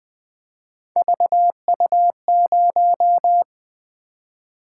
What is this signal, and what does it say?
Morse code, heard as a keyed tone, about 20 wpm, 700 Hz. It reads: VU0
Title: Three Letter-Number Combinations - 20wpm